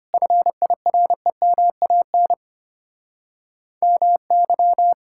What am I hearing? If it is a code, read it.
Morse code, heard as a keyed tone, about 32 wpm, 700 Hz. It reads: FIREMAN MY